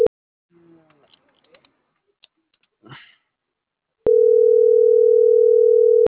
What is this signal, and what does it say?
Ringback tone (North America), 440 + 480 Hz.